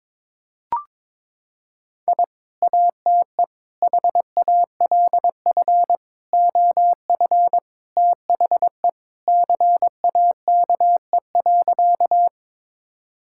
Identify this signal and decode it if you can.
Morse code, heard as a keyed tone, about 22 wpm, 700 Hz. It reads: I ATE HALF OF THE CAKE.
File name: practice / 22wpm / Sentences with 3-Word Phrases - 22wpm